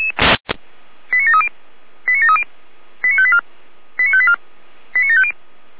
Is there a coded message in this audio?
Yes: five-tone selective call (ZVEI) from 1.1 to 5.3 s, heard as quick runs of five tones, tones of about 70 ms. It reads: 89730 89730 89584 89584 89750